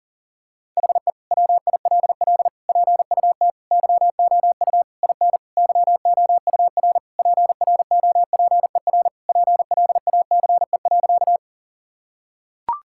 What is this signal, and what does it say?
Morse code, heard as a keyed tone, about 40 wpm, 700 Hz. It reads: HE WILL PUT YOU IN YOUR PROPER PLACE.
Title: Sentences from Top 1000 Words - Review Part 5 - 40wpm